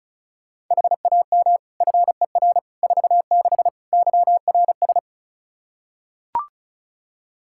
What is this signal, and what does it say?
Morse code, heard as a keyed tone, about 36 wpm, 700 Hz. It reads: HAM FER 46 YRS
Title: Practice QSO - Exchange 3 - More Info - 35wpm